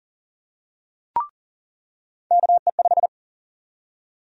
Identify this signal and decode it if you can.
Morse code, heard as a keyed tone, about 40 wpm, 700 Hz. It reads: KE5
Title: USA Call Sign Prefixes - 40wpm